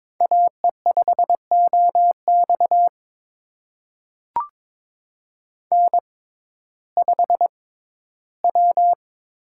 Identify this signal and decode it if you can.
Morse code, heard as a keyed tone, about 22 wpm, spaced as for 14 wpm, 700 Hz. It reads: AE5OX N5W
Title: USA Call Signs - 5 Characters - 6x Character Spacing - 22wpm